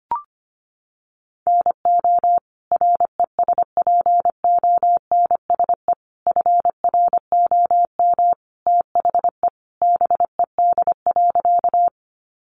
Morse code, heard as a keyed tone, about 25 wpm, 700 Hz. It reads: NO RESPONSE FROM THE BED.
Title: Sentences from Top 1400 Words 25wpm